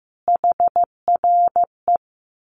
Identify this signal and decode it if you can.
Morse code, heard as a keyed tone, about 15 wpm, 700 Hz. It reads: HRE
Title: Common QSO Phrases 15wpm